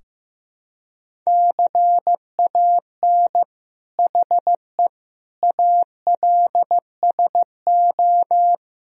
Morse code, heard as a keyed tone, about 15 wpm, 700 Hz. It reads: CAN HE ALSO
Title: Sentences from Top 1000 Words - Review Part 3 - 15wpm